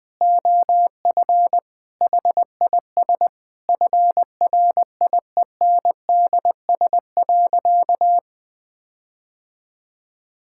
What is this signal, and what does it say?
Morse code, heard as a keyed tone, about 20 wpm, 700 Hz. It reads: OF HIS FRIENDS.
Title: Sentences with 4-Word Phrases - 20wpm